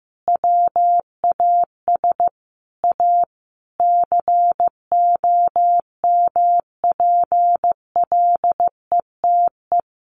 Morse code, heard as a keyed tone, about 15 wpm, 700 Hz. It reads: WAS A COMPLETE
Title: Sentences from Top 1400 Words 15wpm